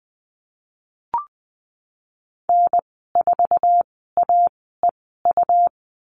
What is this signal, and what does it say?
Morse code, heard as a keyed tone, about 20 wpm, spaced as for 14 wpm, 700 Hz. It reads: N4AEU
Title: 128 Lesson 21g - Number 6 - Call Signs 20wpm